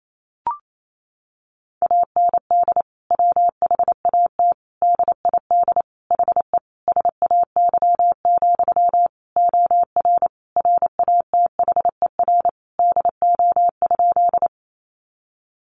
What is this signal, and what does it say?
Morse code, heard as a keyed tone, about 28 wpm, 700 Hz. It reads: AND WHAT DID HE SAY, OR RATHER DO?
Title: Sentences from Top 700 Words 28wpm